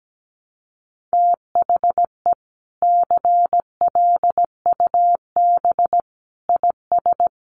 Morse code, heard as a keyed tone, about 17 wpm, 700 Hz. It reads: THE CLUB IS
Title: Sentences from Top 1200 Words 17wpm